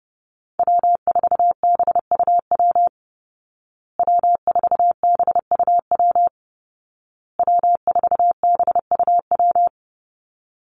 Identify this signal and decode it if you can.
Morse code, heard as a keyed tone, about 28 wpm, 700 Hz. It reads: W4BUW W4BUW W4BUW
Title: Call Signs Repeated 2x - 30wpm